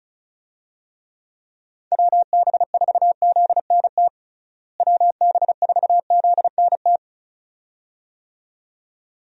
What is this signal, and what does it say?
Morse code, heard as a keyed tone, about 36 wpm, 700 Hz. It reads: WB4ZNT WB4ZNT
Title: USA Call Signs - 6 Characters - Repeated 2x - 35wpm